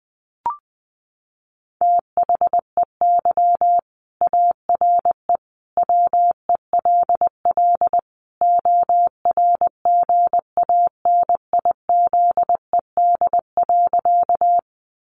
Morse code, heard as a keyed tone, about 20 wpm, 700 Hz. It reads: THEY ARE WELL ORGANIZED.
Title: Sentences from Top 1300 Words 20wpm